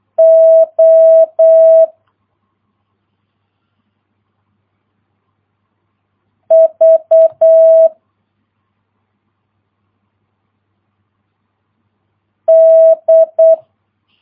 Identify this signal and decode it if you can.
Morse code, heard as a keyed tone, about 8 wpm, 650 Hz. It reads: O V D